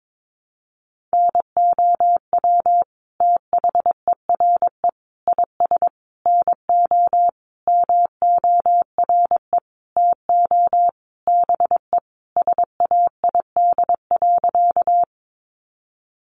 Morse code, heard as a keyed tone, about 22 wpm, 700 Hz. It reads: NOW THERE IS NO MORE TO BE SAID.
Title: Sentences from Top 100 Words 22wpm